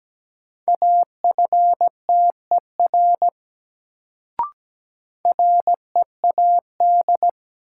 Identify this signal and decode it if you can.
Morse code, heard as a keyed tone, about 17 wpm, 700 Hz. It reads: AFTER READ
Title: Sentences from Top 200 Words - New Word Drill - 17wpm